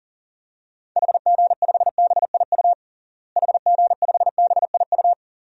Morse code, heard as a keyed tone, about 40 wpm, 700 Hz. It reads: HG5BIU HG5BIU
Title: DXCC-100 Call Signs - 6 Characters - Repeated 3x - 40wpm